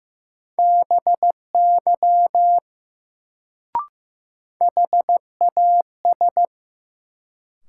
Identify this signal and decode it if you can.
Morse code, heard as a keyed tone, about 15 wpm, 700 Hz. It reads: BY HAS
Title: Sets of 2 Words - New Word Drill - 15wpm